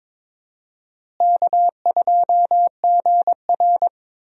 Morse code, heard as a keyed tone, about 22 wpm, 700 Hz. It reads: K2GR
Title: USA Call Signs - 4 Characters - Repeated 2x - 22wpm